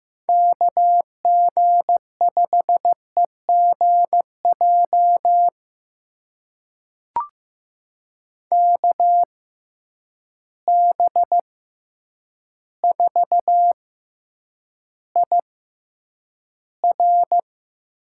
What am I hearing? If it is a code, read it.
Morse code, heard as a keyed tone, about 15 wpm, spaced as for 8 wpm, 700 Hz. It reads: KG5EGJ KB4IR
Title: USA Call Signs - 6 Characters - 6x Character Spacing - 15wpm